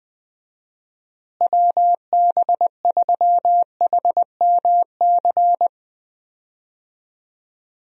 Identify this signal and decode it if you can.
Morse code, heard as a keyed tone, about 21 wpm, 700 Hz. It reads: WB3HMC
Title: USA Call Signs - 6 Characters - Repeated 2x - 20wpm